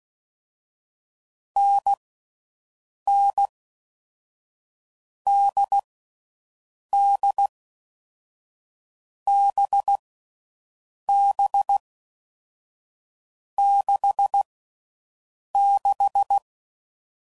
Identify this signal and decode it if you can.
Morse code, heard as a keyed tone, about 16 wpm, 800 Hz. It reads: N N D D B B 6 6
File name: Mixed / Short Runs / awjndb6gz7